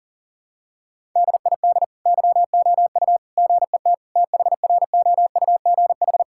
Morse code, heard as a keyed tone, about 40 wpm, 700 Hz. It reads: DID YOU GET THROUGH
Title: Sentences from Top 1000 Words - Review Part 5 - 40wpm